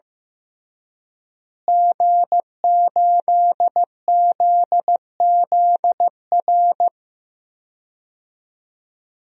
Morse code, heard as a keyed tone, about 15 wpm, 700 Hz. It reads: G8ZZR
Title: DXCC-100 Call Signs - 5 Characters - Repeated 3x - 15wpm